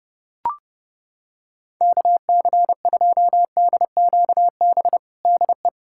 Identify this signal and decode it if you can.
Morse code, heard as a keyed tone, about 30 wpm, 700 Hz. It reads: KC2DQB DE
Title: USA Callsign DE Callsign 30wpm